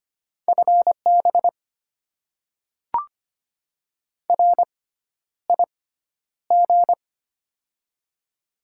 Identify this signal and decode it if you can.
Morse code, heard as a keyed tone, about 25 wpm, spaced as for 11 wpm, 700 Hz. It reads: FB RIG
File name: Top 20 QSO Elements - 6x Character Spacing - 25wpm